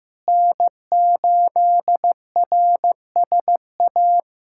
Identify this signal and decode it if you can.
Morse code, heard as a keyed tone, about 16 wpm, 700 Hz. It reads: N8RSA
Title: USA Callsign DE Callsign - 5 Characters - 15wpm